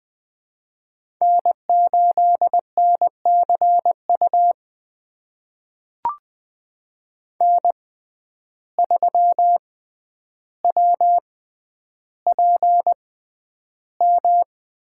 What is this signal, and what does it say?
Morse code, heard as a keyed tone, about 20 wpm, spaced as for 10 wpm, 700 Hz. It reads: N8NCU N3WPM
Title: USA Call Signs - 5 Characters - 6x Character Spacing - 20wpm